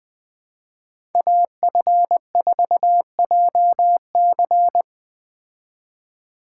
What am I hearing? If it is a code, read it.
Morse code, heard as a keyed tone, about 20 wpm, 700 Hz. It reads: AF4JC